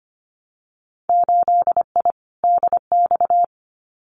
Morse code, heard as a keyed tone, about 25 wpm, 690 Hz. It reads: 8I DX
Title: Winter Field Day Exchanges Encore 25wpm